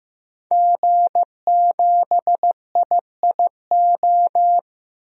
Morse code, heard as a keyed tone, about 15 wpm, 700 Hz. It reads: G7IIO
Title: DXCC-100 Call Signs - 5 Characters - 15wpm